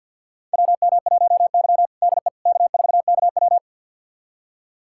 Morse code, heard as a keyed tone, about 51 wpm, 700 Hz. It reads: WM1Y DE K4KW
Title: USA Callsign DE Callsign - 4 Characters - 50wpm